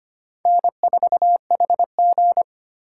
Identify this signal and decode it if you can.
Morse code, heard as a keyed tone, about 25 wpm, 700 Hz. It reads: N4HG
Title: USA Call Signs - 3x Character Spacing - 25wpm